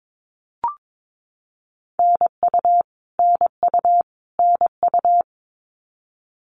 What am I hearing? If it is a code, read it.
Morse code, heard as a keyed tone, about 22 wpm, 700 Hz. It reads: NU NU NU
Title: Canadian Provinces and Territories - Repeated 3x - 22wpm